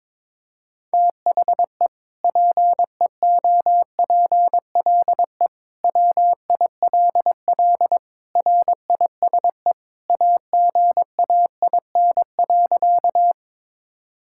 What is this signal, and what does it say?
Morse code, heard as a keyed tone, about 21 wpm, 700 Hz. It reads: THE PEOPLE WILL RISE AGAIN.
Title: Sentences from Top 1000 Words - Review Part 6 - 22wpm